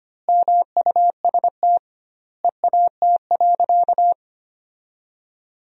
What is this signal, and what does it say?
Morse code, heard as a keyed tone, about 25 wpm, 700 Hz. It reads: MUST EAT.
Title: 2x Word-Spacing - Sentences from Top 300 Words - 25wpm